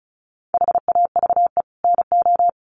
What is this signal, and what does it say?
Morse code, heard as a keyed tone, about 34 wpm, 700 Hz. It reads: HAVE NO
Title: Sets of 5 Words Encore 35wpm